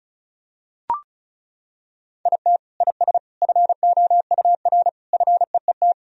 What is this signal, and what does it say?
Morse code, heard as a keyed tone, about 36 wpm, 690 Hz. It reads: IT IS FOUR FEET
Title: Sentences from Top 1000 Words - Review Part 1 - 35wpm